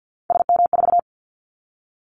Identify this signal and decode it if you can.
Morse code, heard as a keyed tone, about 50 wpm, 700 Hz. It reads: SM4
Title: Callsign Prefixes for Top 25 DXCC - 50wpm